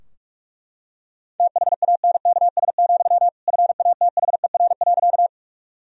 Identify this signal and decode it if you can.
Morse code, heard as a keyed tone, about 41 wpm, 700 Hz. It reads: THANKS, FATHER.
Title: Sentences from Top 1000 Words - Review Part 3 - 45wpm